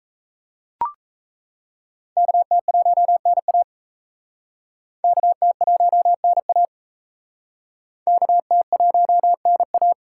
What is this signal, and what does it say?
Morse code, heard as a keyed tone, about 38 wpm, 700 Hz. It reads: KT1NA KT1NA KT1NA